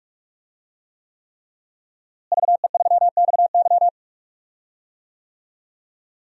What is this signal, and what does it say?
Morse code, heard as a keyed tone, about 46 wpm, 700 Hz. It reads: VE3XY
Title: DXCC-100 Call Signs - 5 Characters - Repeated 3x - 45wpm